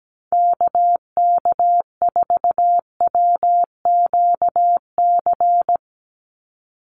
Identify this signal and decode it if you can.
Morse code, heard as a keyed tone, about 17 wpm, 700 Hz. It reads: KK4WQC
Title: USA Call Signs 17wpm